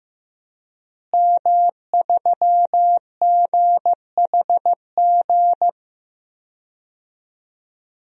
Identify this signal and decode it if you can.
Morse code, heard as a keyed tone, about 15 wpm, 700 Hz. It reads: M3GHG
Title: DXCC-100 Call Signs - 5 Characters - Repeated 2x - 15wpm